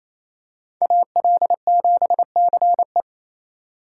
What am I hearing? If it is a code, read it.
Morse code, heard as a keyed tone, about 28 wpm, 700 Hz. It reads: AL7CE
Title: USA Call Signs - 2x Character Spacing - 28wpm